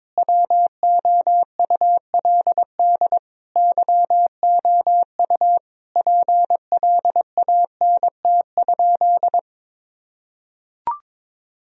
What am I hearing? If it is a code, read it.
Morse code, heard as a keyed tone, about 22 wpm, 700 Hz. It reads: WOULD YOU PLANT?
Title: Sentences from Top 1000 Words - Review Part 1 - 22wpm